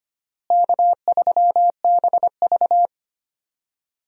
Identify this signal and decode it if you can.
Morse code, heard as a keyed tone, about 25 wpm, 700 Hz. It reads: K3BV
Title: USA Call Signs - 4 Characters - Repeated 3x - 25wpm